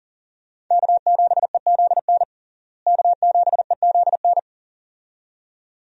Morse code, heard as a keyed tone, about 40 wpm, 700 Hz. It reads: K7EZN K7EZN
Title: USA Call Signs - 5 Characters - Repeated 2x - 40wpm